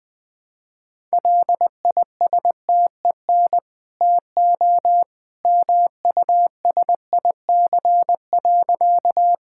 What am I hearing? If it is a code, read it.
Morse code, heard as a keyed tone, about 20 wpm, 700 Hz. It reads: LISTEN TO MUSIC.
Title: Easy Sentences with Hints - 20wpm